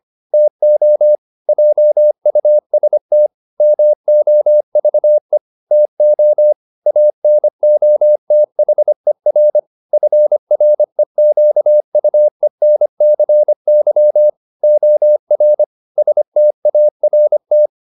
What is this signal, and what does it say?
Morse code, heard as a keyed tone, about 25 wpm, 600 Hz. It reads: TO JUST MOVE TO ANOTHER FREQUENCY OR START